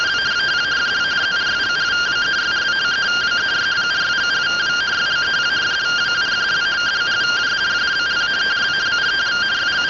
Two-tone teleprinter signal, about 170 Hz shift.